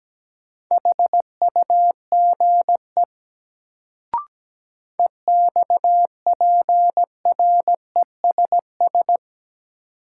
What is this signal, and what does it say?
Morse code, heard as a keyed tone, about 17 wpm, 700 Hz. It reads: HUGE EXPRESS
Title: Sentences from Top 1100 Words - New Word Drill - 17wpm